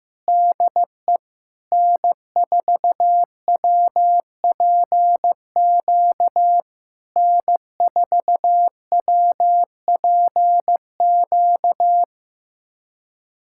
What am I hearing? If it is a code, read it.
Morse code, heard as a keyed tone, about 15 wpm, 700 Hz. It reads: DE N4WPQ N4WPQ
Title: SOTA QSO Practice 15wpm